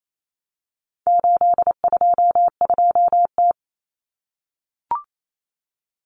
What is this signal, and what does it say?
Morse code, heard as a keyed tone, about 28 wpm, 700 Hz. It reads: 822T
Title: Common Cut Numbers - 4-Digit 28wpm